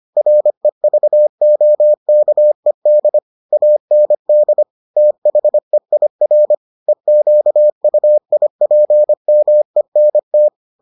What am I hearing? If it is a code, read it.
Morse code, heard as a keyed tone, about 25 wpm, 600 Hz. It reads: REVOKED AND THEIR EQUIPMENT